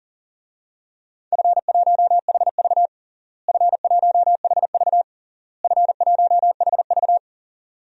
Morse code, heard as a keyed tone, about 40 wpm, 700 Hz. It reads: F1HV F1HV F1HV
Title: DXCC-100 Call Signs - 4 Characters - Repeated 3x - 40wpm